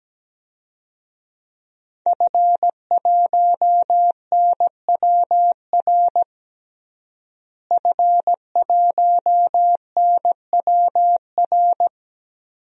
Morse code, heard as a keyed tone, about 17 wpm, 700 Hz. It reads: F1NWR F1NWR
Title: DXCC-100 Call Signs - 5 Characters - Repeated 2x - 17wpm